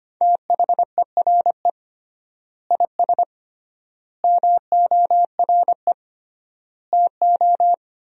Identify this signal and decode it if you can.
Morse code, heard as a keyed tone, about 25 wpm, 700 Hz. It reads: THERE IS MORE TO
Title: 3x Word-Spacing - Sentences from Top 200 Words - 25wpm